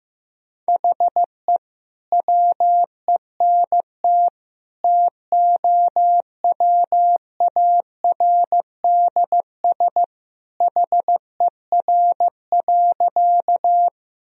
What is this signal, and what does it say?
Morse code, heard as a keyed tone, about 15 wpm, 700 Hz. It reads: HE WENT TOWARDS HER.
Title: Sentences from Top 1000 Words - Review Part 2 - 15wpm